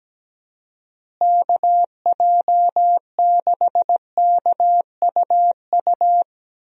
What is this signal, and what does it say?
Morse code, heard as a keyed tone, about 17 wpm, 700 Hz. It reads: KJ6KUU
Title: USA Call Signs - 6 Characters - Repeated 3x - 17wpm